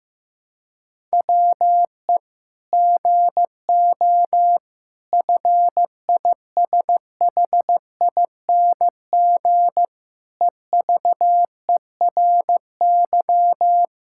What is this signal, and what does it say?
Morse code, heard as a keyed tone, about 15 wpm, 700 Hz. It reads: WE GO FISHING EVERY
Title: Sentences from Top 1000 Words - Review Part 5 - 15wpm